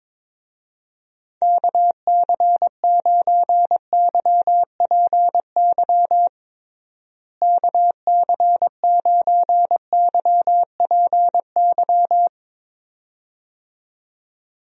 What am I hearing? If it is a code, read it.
Morse code, heard as a keyed tone, about 22 wpm, 700 Hz. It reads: KC9YPY KC9YPY
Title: USA Call Signs - 6 Characters - Repeated 2x - 22wpm